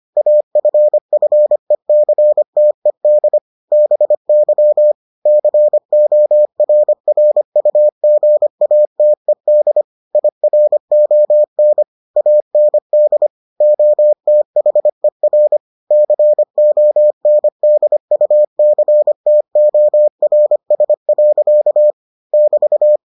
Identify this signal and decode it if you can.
Morse code, heard as a keyed tone, about 25 wpm, 600 Hz. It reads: AFFECTED BY CORRUGATED IRON AND OTHER CONDUCTORS. <BT>